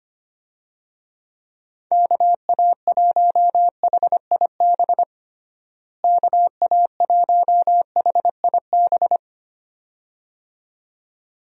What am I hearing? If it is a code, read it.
Morse code, heard as a keyed tone, about 25 wpm, 700 Hz. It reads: KA1HIB KA1HIB
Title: USA Call Signs - 6 Characters - Repeated 2x - 25wpm